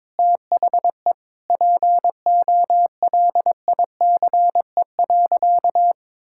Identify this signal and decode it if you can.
Morse code, heard as a keyed tone, about 22 wpm, 700 Hz. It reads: THE POLICE.
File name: Sentences with 7-Word Phrases - 22wpm